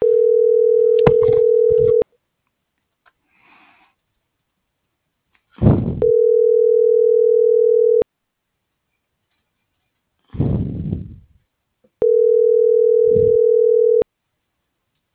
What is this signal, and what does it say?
Ringback tone (North America), 440 + 480 Hz.